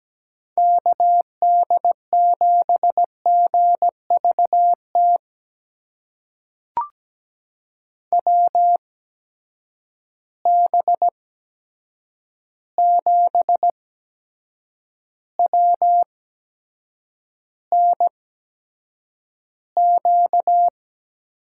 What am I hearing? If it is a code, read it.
Morse code, heard as a keyed tone, about 17 wpm, spaced as for 7 wpm, 700 Hz. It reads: KD7GVT WB7WNQ
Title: USA Call Signs - 6 Characters - 8x Character Spacing - 17wpm